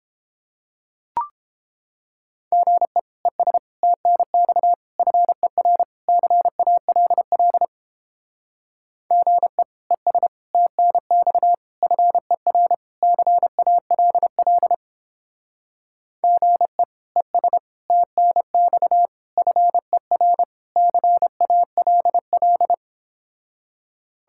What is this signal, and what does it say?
Morse code, heard as a keyed tone, about 29 wpm, 700 Hz. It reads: GE ES TNX FER CALL GE ES TNX FER CALL GE ES TNX FER CALL